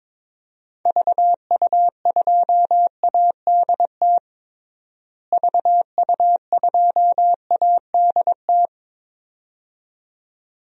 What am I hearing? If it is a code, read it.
Morse code, heard as a keyed tone, about 22 wpm, 700 Hz. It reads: VU2ADT VU2ADT
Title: DXCC-100 Call Signs - 6 Characters - Repeated 2x - 22wpm